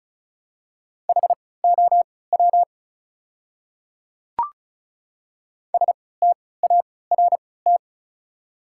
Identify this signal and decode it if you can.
Morse code, heard as a keyed tone, about 34 wpm, spaced as for 20 wpm, 700 Hz. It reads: HOW START
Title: 3x Character Spacing - Top 300 Words - 35wpm